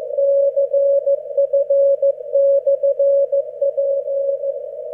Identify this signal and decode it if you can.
Morse code, heard as a keyed tone, about 15 wpm, 550 Hz. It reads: CF/P